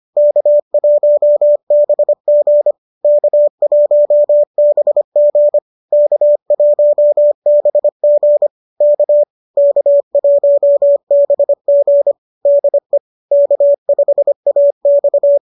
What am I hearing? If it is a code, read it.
Morse code, heard as a keyed tone, about 25 wpm, 590 Hz. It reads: K1BG K1BG K1BG K K1BG DE K5AX